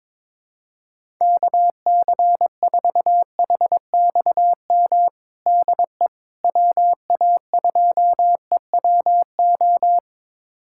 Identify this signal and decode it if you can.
Morse code, heard as a keyed tone, about 22 wpm, 700 Hz. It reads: KC4HXM DE WA2EWO
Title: USA Callsign DE Callsign 22wpm